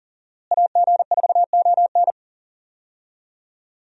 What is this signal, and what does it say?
Morse code, heard as a keyed tone, about 37 wpm, 700 Hz. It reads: AG4ON
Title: USA Call Signs - 5 Characters - Repeated 2x - 40wpm